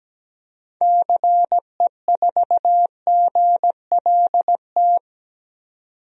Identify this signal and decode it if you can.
Morse code, heard as a keyed tone, about 17 wpm, 700 Hz. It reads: CE4GLT
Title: DXCC-100 Call Signs - 6 Characters - 17wpm